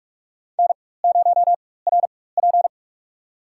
Morse code, heard as a keyed tone, about 46 wpm, spaced as for 21 wpm, 700 Hz. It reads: N0RP
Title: USA Call Signs - 4 Characters - 4x Character Spacing - 45wpm